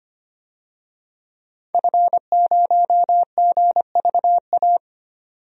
Morse code, heard as a keyed tone, about 25 wpm, 700 Hz. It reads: F0GVA
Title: DXCC-100 Call Signs - 5 Characters - 25wpm